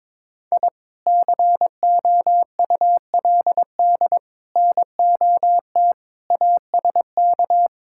Morse code, heard as a keyed tone, about 21 wpm, 700 Hz. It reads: I COULD NOT ASK
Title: Sentences from Top 1000 Words - Review Part 5 - 22wpm